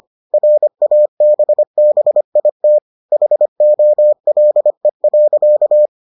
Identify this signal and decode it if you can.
Morse code, heard as a keyed tone, about 25 wpm, 600 Hz. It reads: RABBIT HOLE.